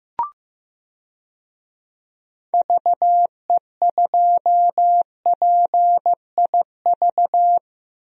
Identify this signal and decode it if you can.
Morse code, heard as a keyed tone, about 15 wpm, 700 Hz. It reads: VE2PIV
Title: DXCC-100 Call Signs - 6 Characters - Repeated 3x - 15wpm